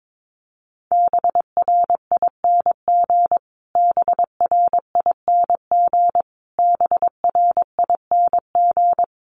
Morse code, heard as a keyed tone, about 22 wpm, 700 Hz. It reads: BRING BRING BRING
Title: Top 500 Words Repeated 22wpm